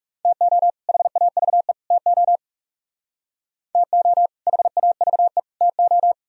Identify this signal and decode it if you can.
Morse code, heard as a keyed tone, about 36 wpm, 700 Hz. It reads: TO HAVE TO TO HAVE TO
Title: Speed-Racing - Sets of 3 Words 30wpm